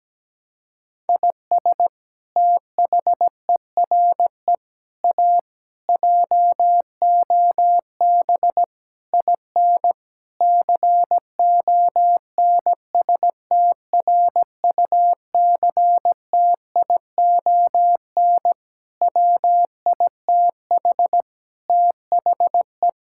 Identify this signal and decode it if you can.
Morse code, heard as a keyed tone, about 17 wpm, 700 Hz. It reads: IS THERE A JOB IN CONSTRUCTION WITH THE
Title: Sentences from Top 1600 Words 17wpm